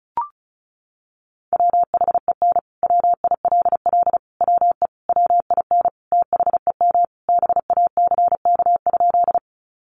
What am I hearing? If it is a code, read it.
Morse code, heard as a keyed tone, about 34 wpm, 700 Hz. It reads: WHEN WILL WE WIN THEM BACK?
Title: Sentences from Top 1500 Words 35wpm